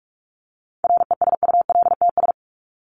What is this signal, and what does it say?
Morse code, heard as a keyed tone, about 44 wpm, 700 Hz. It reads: RESULTS
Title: ES:S Suffix - Top 100 Words with Base Words 45wpm